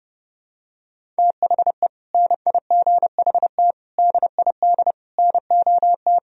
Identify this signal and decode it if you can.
Morse code, heard as a keyed tone, about 32 wpm, 700 Hz. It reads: THE NIGHT DID NOT
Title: Sentences from Top 1000 Words - Review Part 1 - 30wpm